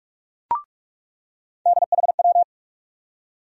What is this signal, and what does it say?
Morse code, heard as a keyed tone, about 42 wpm, 700 Hz. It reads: DHW
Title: USA Call Sign Postfixes - 3 Letters 45wpm